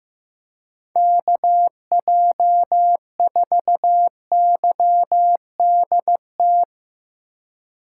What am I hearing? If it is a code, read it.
Morse code, heard as a keyed tone, about 16 wpm, 700 Hz. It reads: KJ4YDT
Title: Speed Racing - USA 2x3 Call Signs 15wpm